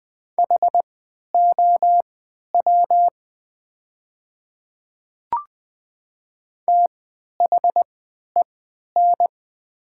Morse code, heard as a keyed tone, about 20 wpm, spaced as for 11 wpm, 700 Hz. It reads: HOW THEN
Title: 3x Character Spacing - Top 100 Words - 20wpm